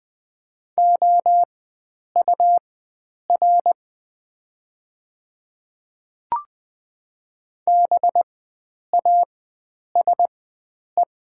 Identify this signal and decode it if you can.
Morse code, heard as a keyed tone, about 20 wpm, spaced as for 9 wpm, 700 Hz. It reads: OUR BASE
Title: 4x Character Spacing - Top 500 Words - 20wpm